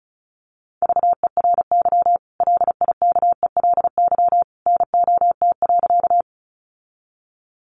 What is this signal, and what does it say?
Morse code, heard as a keyed tone, about 34 wpm, 700 Hz. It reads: VERY LIKELY NOT.
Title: Sentences from Top 600 Words 35wpm